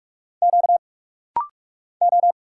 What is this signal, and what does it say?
Morse code, heard as a keyed tone, about 44 wpm, 700 Hz. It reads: Q O
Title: Single Letter-Number Alt - Rapid-Fire 45wpm